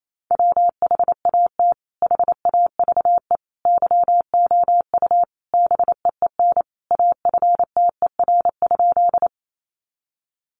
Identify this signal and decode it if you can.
Morse code, heard as a keyed tone, about 28 wpm, 700 Hz. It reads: WHAT HAVE YOU BEEN AFTER?
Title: Sentences from Top 200 Words 28wpm